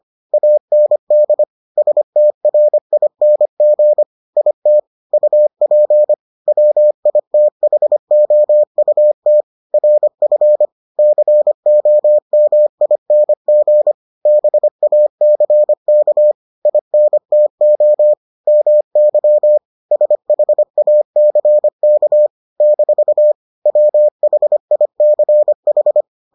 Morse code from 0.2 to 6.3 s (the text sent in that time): AND STRING IT UP